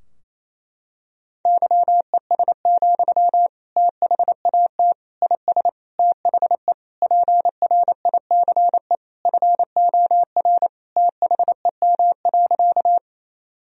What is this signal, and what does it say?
Morse code, heard as a keyed tone, about 27 wpm, 700 Hz. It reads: YES, THAT IS THE PRICE FOR THEM.
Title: Sentences from Top 1000 Words - Review Part 4 - 28wpm